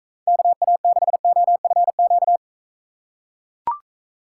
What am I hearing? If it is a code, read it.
Morse code, heard as a keyed tone, about 42 wpm, 700 Hz. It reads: KA6OFQ